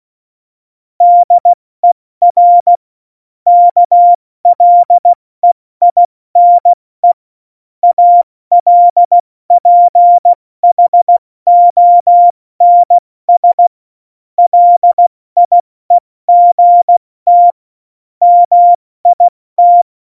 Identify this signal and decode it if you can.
Morse code, heard as a keyed tone, about 16 wpm, 700 Hz. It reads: DER KLEINE ALPHONS LIEGT MIT
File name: Morsekurs / Witze / Witz 30. Januar 2018, 70 F80